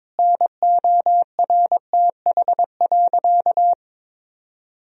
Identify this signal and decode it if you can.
Morse code, heard as a keyed tone, about 22 wpm, 700 Hz. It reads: NORTH.
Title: Sentences from Top 1000 Words - Review Part 2 - 22wpm